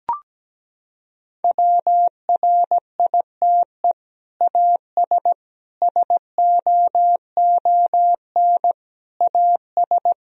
Morse code, heard as a keyed tone, about 17 wpm, 700 Hz. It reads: WRITE AS SOON AS
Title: Sentences from Top 1000 Words - Review Part 1 - 17wpm